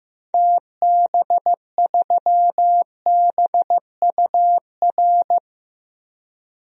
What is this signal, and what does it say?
Morse code, heard as a keyed tone, about 15 wpm, 700 Hz. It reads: TB3BUR
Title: DXCC-100 Call Signs - 6 Characters - 15wpm